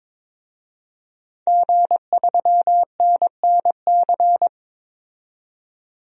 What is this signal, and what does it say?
Morse code, heard as a keyed tone, about 22 wpm, 700 Hz. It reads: G3NNC
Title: DXCC-100 Call Signs - 5 Characters - 22wpm